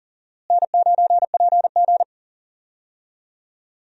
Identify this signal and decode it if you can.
Morse code, heard as a keyed tone, about 40 wpm, 700 Hz. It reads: N9PG
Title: USA Call Signs - 4 Characters - Repeated 2x - 40wpm